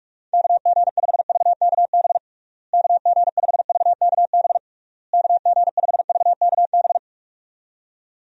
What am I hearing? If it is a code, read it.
Morse code, heard as a keyed tone, about 42 wpm, 700 Hz. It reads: KG5VKB KG5VKB KG5VKB
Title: USA Call Signs - 6 Characters - Repeated 3x - 45wpm